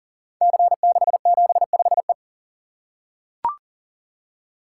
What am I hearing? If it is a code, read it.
Morse code, heard as a keyed tone, about 38 wpm, 690 Hz. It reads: C675E
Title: Code Groups - 5 Letter:Numbers - 40wpm